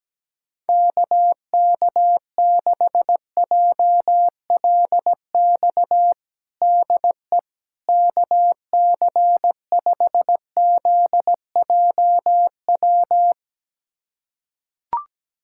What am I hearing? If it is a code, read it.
Morse code, heard as a keyed tone, about 17 wpm, 700 Hz. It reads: KK6JLX DE KC5ZJW
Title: USA Callsign DE Callsign 17wpm